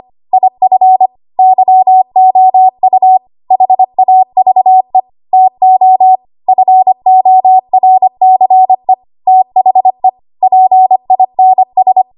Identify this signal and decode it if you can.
Morse code, heard as a keyed tone, about 25 wpm, 750 Hz. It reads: IF YOU HAVE TO FORCE THE PINS
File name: Archive / 25 WPM / 25WPM